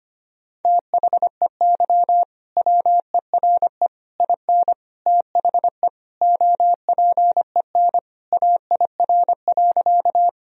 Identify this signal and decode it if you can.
Morse code, heard as a keyed tone, about 25 wpm, 700 Hz. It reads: THEY WERE IN THE OPEN AIR.
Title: Sentences from Top 1000 Words - Review Part 1 - 25wpm